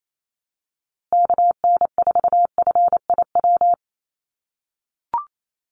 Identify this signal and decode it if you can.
Morse code, heard as a keyed tone, about 28 wpm, 700 Hz. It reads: KN4FIW
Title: USA Call Signs Repeated 2x 28wpm